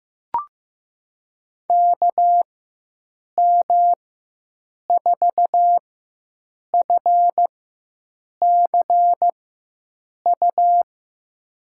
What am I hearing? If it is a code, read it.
Morse code, heard as a keyed tone, about 15 wpm, spaced as for 7 wpm, 700 Hz. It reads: KM4FCU
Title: USA Call Signs - 6 Characters - 4x Character Spacing - 15wpm